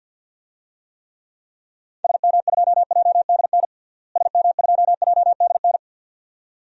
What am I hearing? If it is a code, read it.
Morse code, heard as a keyed tone, about 47 wpm, 700 Hz. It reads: SM2JDN SM2JDN
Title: DXCC-100 Call Signs - 6 Characters - Repeated 2x - 50wpm